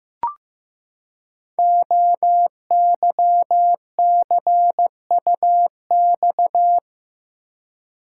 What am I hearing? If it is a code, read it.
Morse code, heard as a keyed tone, about 15 wpm, 700 Hz. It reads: OYCUX
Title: Code Groups - 5 Letter:Numbers - 15wpm